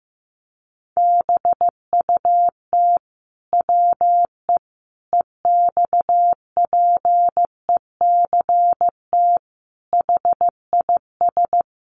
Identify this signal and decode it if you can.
Morse code, heard as a keyed tone, about 15 wpm, 700 Hz. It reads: BUT WE EXPECT HIS
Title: Sentences from Top 1100 Words 15wpm